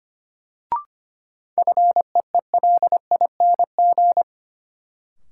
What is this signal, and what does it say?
Morse code, heard as a keyed tone, about 25 wpm, 700 Hz. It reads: FEELING
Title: Sentences from Top 1000 Words - Review Part 3 - New Word Drill - 25wpm